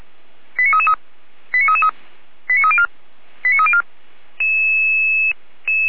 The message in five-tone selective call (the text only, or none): 89333 89333 89304 89304